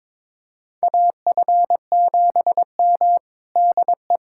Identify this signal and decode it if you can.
Morse code, heard as a keyed tone, about 22 wpm, 700 Hz. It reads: AF7M DE
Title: USA Callsign DE Callsign - 4 Characters - 22wpm